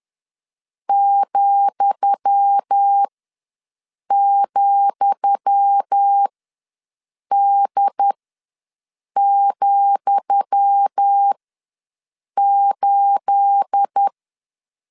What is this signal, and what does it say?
Morse code, heard as a keyed tone, about 10 wpm, spaced as for 6 wpm, 800 Hz. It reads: ,,D,8
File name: LEZIONI / 10 SIGNS / 10D COMA ws 10wpm 4314